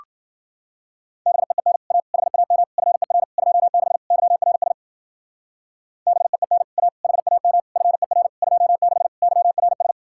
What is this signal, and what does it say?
Morse code, heard as a keyed tone, about 50 wpm, 700 Hz. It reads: BEEN A HAM FER 26 YRS BEEN A HAM FER 26 YRS
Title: Speed-Racing Common QSO Phrases Encore - 40wpm